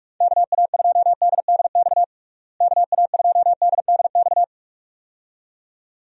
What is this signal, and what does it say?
Morse code, heard as a keyed tone, about 44 wpm, 700 Hz. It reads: KA2DDX KA2DDX
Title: USA Call Signs - 6 Characters - Repeated 2x - 45wpm